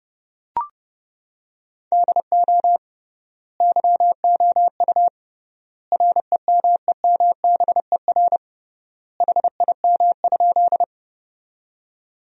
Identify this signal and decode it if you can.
Morse code, heard as a keyed tone, about 30 wpm, 700 Hz. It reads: DO YOU REMEMBER HIM?
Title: 3x Word-Spacing - Sentences from Top 400 Words - 30wpm